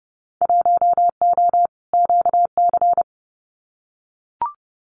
Morse code, heard as a keyed tone, about 30 wpm, 700 Hz. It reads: 1O QC
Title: Winter Field Day Exchanges 30wpm